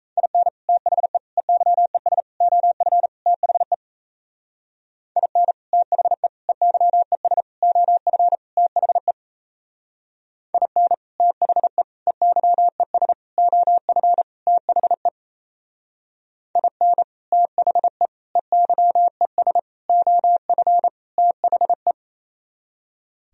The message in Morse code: IN THE EYES OF THE IN THE EYES OF THE IN THE EYES OF THE IN THE EYES OF THE

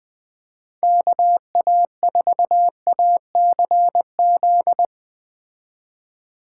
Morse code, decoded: KA4ACZ

